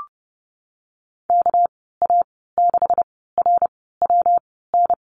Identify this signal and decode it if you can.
Morse code, heard as a keyed tone, about 30 wpm, spaced as for 17 wpm, 700 Hz. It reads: KA6RWN
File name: Lesson 37g - Letter x - Call Signs 30wpm